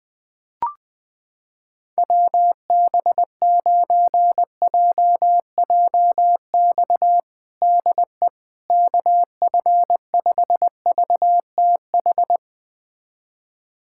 Morse code, heard as a keyed tone, about 19 wpm, 700 Hz. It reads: WB9JJX DE KF5VTH
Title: USA Callsign DE Callsign - 6 Characters - 20wpm